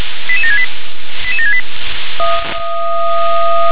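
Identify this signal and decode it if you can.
Five-tone selective call (ZVEI), heard as quick runs of five tones, tones of about 70 ms. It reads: 88768 88768
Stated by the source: ZVEI five-tone selective call received by SDR.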